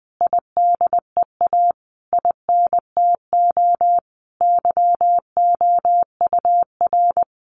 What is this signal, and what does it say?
Morse code, heard as a keyed tone, about 20 wpm, 700 Hz. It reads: IDEA INTO YOUR
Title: Sentences from Top 600 Words 20wpm